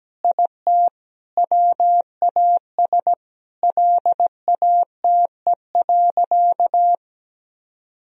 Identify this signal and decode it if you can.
Morse code, heard as a keyed tone, about 17 wpm, 700 Hz. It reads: IT WAS LATE.
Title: Sentences from Top 1000 Words - Review Part 3 - 17wpm